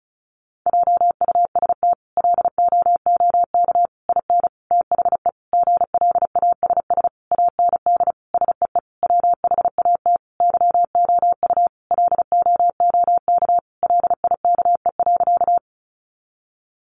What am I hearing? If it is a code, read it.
Morse code, heard as a keyed tone, about 34 wpm, 700 Hz. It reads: JUST LOOK IN THE GLASS AND SEE WHAT YOU LOOK LIKE.
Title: Sentences from Top 1300 Words 35wpm